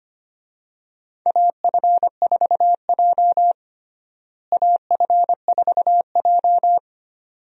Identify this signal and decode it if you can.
Morse code, heard as a keyed tone, about 25 wpm, 700 Hz. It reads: AF4J AF4J
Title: USA Call Signs - 4 Characters - Repeated 2x - 25wpm